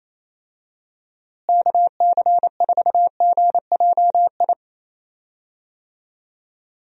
Morse code, heard as a keyed tone, about 27 wpm, 700 Hz. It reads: KC4GJI
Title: USA Call Signs - 6 Characters - Repeated 2x - 28wpm